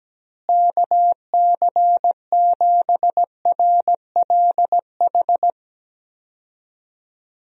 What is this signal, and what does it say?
Morse code, heard as a keyed tone, about 17 wpm, 700 Hz. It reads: KC7RLH
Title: USA Call Signs - 6 Characters - Repeated 3x - 17wpm